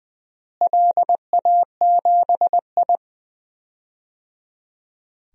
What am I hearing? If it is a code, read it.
Morse code, heard as a keyed tone, about 20 wpm, 700 Hz. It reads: LA7I
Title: DXCC-100 Call Signs - 4 Characters - Repeated 3x - 20wpm